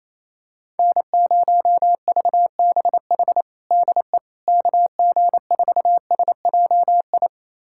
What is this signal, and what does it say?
Morse code, heard as a keyed tone, about 28 wpm, 700 Hz. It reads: N0VBH DE KG4SJI
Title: USA Callsign DE Callsign 28wpm